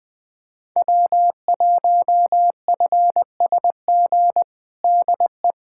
Morse code, heard as a keyed tone, about 21 wpm, 700 Hz. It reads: W1FSG DE
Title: USA Callsign DE Callsign - 5 Characters - 20wpm